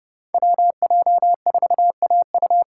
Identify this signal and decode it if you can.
Morse code, heard as a keyed tone, about 30 wpm, 700 Hz. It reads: WJ4AU
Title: Code Groups - 5 Letter:Numbers - 30wpm